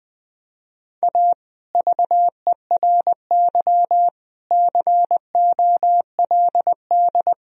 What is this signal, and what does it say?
Morse code, heard as a keyed tone, about 21 wpm, 700 Hz. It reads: A VERY COLD
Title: Sentences from Top 1000 Words - Review Part 1 - 20wpm